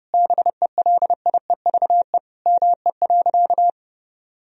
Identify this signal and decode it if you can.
Morse code, heard as a keyed tone, about 30 wpm, 700 Hz. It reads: BELIEVE ME.
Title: Sentences from Top 1000 Words - Review Part 3 - 30wpm